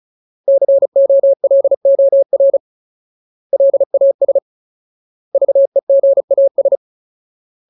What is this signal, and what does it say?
Morse code, heard as a keyed tone, about 34 wpm, 550 Hz. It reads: COLOR LAS VEGAS
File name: Ultimate Word List / 35WPM Ultimate